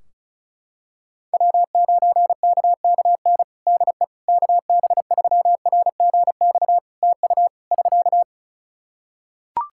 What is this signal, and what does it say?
Morse code, heard as a keyed tone, about 36 wpm, 700 Hz. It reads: W9KKN DE KB3RGX TU <SK>
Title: Practice QSO - Exchange 4 - Ending - 35wpm